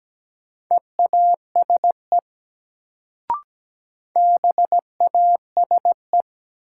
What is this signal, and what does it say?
Morse code, heard as a keyed tone, about 17 wpm, 700 Hz. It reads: EASE BASE
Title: Sentences from Top 1000 Words - Review Part 1 - New Word Drill - 17wpm